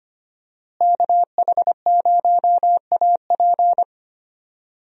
Morse code, heard as a keyed tone, about 25 wpm, 700 Hz. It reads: KH0AP
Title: USA Call Signs - 5 Characters - Repeated 3x - 25wpm